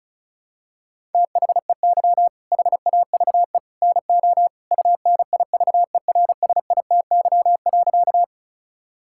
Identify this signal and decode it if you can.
Morse code, heard as a keyed tone, about 34 wpm, 700 Hz. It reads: THEY HAVE NO UNIVERSITY.
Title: Sentences from Top 1000 Words - Review Part 5 - 35wpm